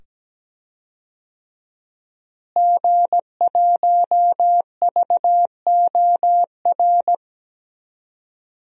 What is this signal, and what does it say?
Morse code, heard as a keyed tone, about 17 wpm, 700 Hz. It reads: G1VOR
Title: DXCC-100 Call Signs - 5 Characters - Repeated 2x - 17wpm